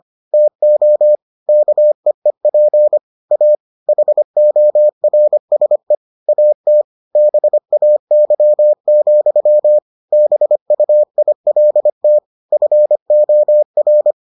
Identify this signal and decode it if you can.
Morse code, heard as a keyed tone, about 25 wpm, 600 Hz. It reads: TO KEEP A HORSE AT BAY, BUILT FOR